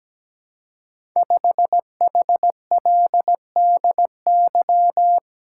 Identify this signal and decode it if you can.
Morse code, heard as a keyed tone, about 17 wpm, 700 Hz. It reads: 5HLDY
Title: Code Groups - 5 Letter:Numbers - 17wpm